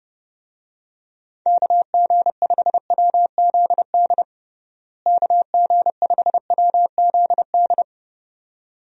Morse code, heard as a keyed tone, about 30 wpm, 700 Hz. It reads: KG5WZD KG5WZD
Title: USA Call Signs - 6 Characters - Repeated 2x - 30wpm